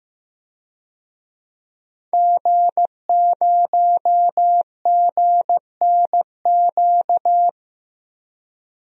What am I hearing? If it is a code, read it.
Morse code, heard as a keyed tone, about 15 wpm, 700 Hz. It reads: G0GNQ
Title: DXCC-100 Call Signs - 5 Characters - 15wpm